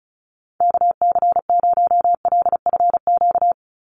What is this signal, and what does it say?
Morse code, heard as a keyed tone, about 34 wpm, 700 Hz. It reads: KC0LFQ
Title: USA Call Signs 35wpm